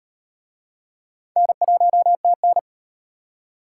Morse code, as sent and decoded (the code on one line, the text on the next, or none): -. .---- - -.
N1TN